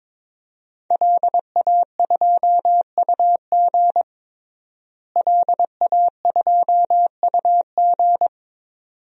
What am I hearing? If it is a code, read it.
Morse code, heard as a keyed tone, about 22 wpm, 700 Hz. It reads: LA2UG LA2UG
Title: DXCC-100 Call Signs - 5 Characters - Repeated 2x - 22wpm